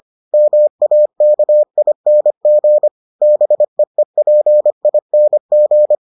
Morse code, heard as a keyed tone, about 25 wpm, 600 Hz. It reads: MAKING BEEPING